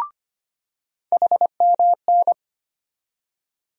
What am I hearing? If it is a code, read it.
Morse code, heard as a keyed tone, about 25 wpm, 700 Hz. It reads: HMN
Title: USA Call Sign Postfixes - 3 Letters 25wpm